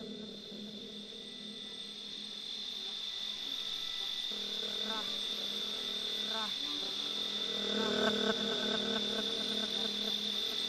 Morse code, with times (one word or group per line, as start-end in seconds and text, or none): none